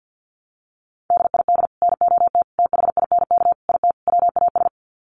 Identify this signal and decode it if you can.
Morse code, heard as a keyed tone, about 50 wpm, 700 Hz. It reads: DID NOT THINK IT WAS
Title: Sets of 5 Words Encore 50wpm